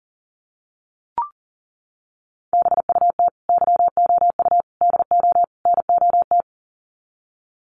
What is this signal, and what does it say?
Morse code, heard as a keyed tone, about 40 wpm, 700 Hz. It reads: BUT YOU DO NOT
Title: Sets of 4 Words 40wpm